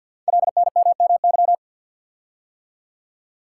Morse code, heard as a keyed tone, about 53 wpm, 700 Hz. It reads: LNMMY
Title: Code Groups - 5 Letters - 50wpm